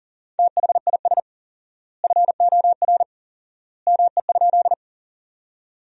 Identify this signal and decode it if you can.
Morse code, heard as a keyed tone, about 40 wpm, 700 Hz. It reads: THIS FOR ME?
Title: 4x Word-Spacing - Sentences from Top 200 Words - 40wpm